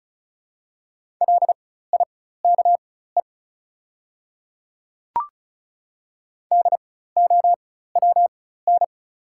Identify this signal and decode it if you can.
Morse code, heard as a keyed tone, about 34 wpm, spaced as for 16 wpm, 690 Hz. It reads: LIKE DOWN
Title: 4x Character Spacing - Top 100 Words - 35wpm